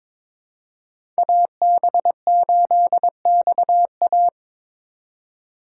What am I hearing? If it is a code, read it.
Morse code, heard as a keyed tone, about 22 wpm, 700 Hz. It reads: AB8XA
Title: USA Call Signs - 5 Characters - 3x Character Spacing - 22wpm